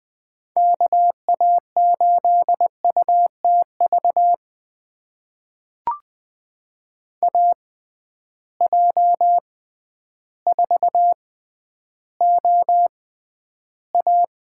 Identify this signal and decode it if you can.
Morse code, heard as a keyed tone, about 20 wpm, spaced as for 11 wpm, 700 Hz. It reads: KA8UTV AJ4OA
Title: USA Call Signs - 6x Character Spacing - 20wpm